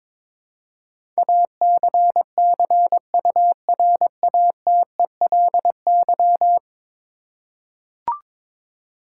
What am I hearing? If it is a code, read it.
Morse code, heard as a keyed tone, about 22 wpm, 700 Hz. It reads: ACCURATELY
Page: morse-code-ninja-practice-22wpm